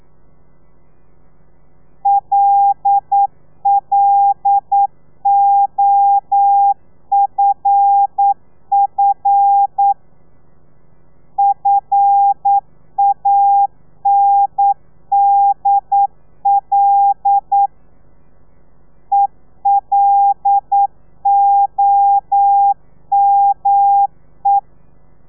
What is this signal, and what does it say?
Morse code, heard as a keyed tone, about 9 wpm, 800 Hz. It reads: LLOFF FANDL ELOME
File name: telegrafia / i2tfj / Lez5 40